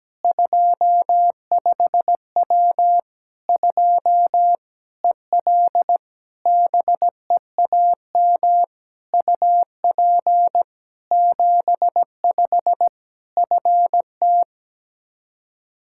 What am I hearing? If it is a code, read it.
Morse code, heard as a keyed tone, about 17 wpm, 700 Hz. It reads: 25W 2 EL BEAM UP 75 FT
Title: Practice QSO - Exchange 2 - Info - 17wpm